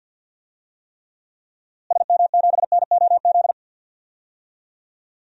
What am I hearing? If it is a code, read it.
Morse code, heard as a keyed tone, about 52 wpm, 700 Hz. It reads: SM7NOZ
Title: DXCC-100 Call Signs - 6 Characters - Repeated 3x - 50wpm